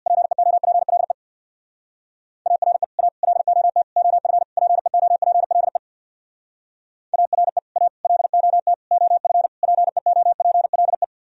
Morse code, heard as a keyed tone, about 66 wpm, 700 Hz. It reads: PEOPLE ARE A LOT OF PEOPLE ARE A LOT OF PEOPLE